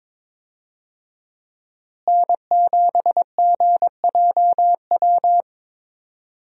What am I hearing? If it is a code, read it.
Morse code, heard as a keyed tone, about 22 wpm, 700 Hz. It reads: N7GJW
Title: USA Call Signs - 5 Characters - Repeated 2x - 22wpm